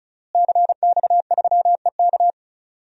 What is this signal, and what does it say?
Morse code, heard as a keyed tone, about 36 wpm, 700 Hz. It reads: CX3EK
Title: DXCC-100 Call Signs - 5 Characters - Repeated 2x - 35wpm